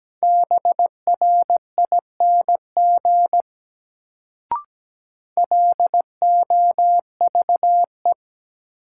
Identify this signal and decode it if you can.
Morse code, heard as a keyed tone, about 17 wpm, 700 Hz. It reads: BRING LOVE
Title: Sets of 5 Words Encore - New Word Drill - 17wpm